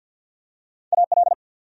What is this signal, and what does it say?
Morse code, heard as a keyed tone, about 53 wpm, 700 Hz. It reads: AL
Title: USA Common Names - Max 3 Letters - 50wpm